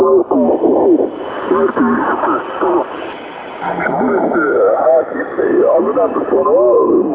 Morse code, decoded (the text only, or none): none